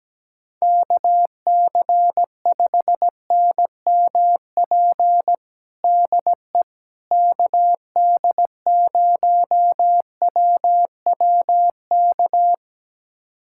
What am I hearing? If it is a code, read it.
Morse code, heard as a keyed tone, about 17 wpm, 700 Hz. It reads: KC5NMP DE KD0WWK
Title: USA Callsign DE Callsign 17wpm